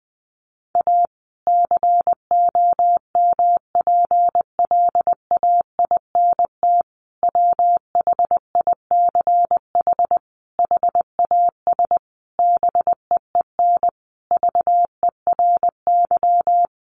Morse code, heard as a keyed tone, about 20 wpm, 700 Hz. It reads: A COMPLAINT WHICH HAS BEEN VERY
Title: Sentences from Top 1400 Words 20wpm